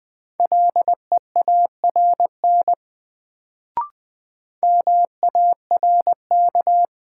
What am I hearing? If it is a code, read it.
Morse code, heard as a keyed tone, about 20 wpm, 700 Hz. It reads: LEARN MARK
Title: Sentences from Top 300 Words - New Word Drill - 20wpm